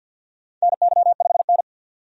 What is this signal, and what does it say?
Morse code, heard as a keyed tone, about 48 wpm, 700 Hz. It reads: NY5N
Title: USA Call Signs - 4 Characters - 50wpm